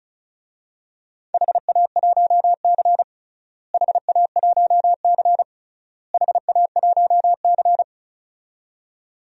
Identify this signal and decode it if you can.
Morse code, heard as a keyed tone, about 36 wpm, 700 Hz. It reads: HA1C HA1C HA1C
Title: DXCC-100 Call Signs - 4 Characters - Repeated 3x - 35wpm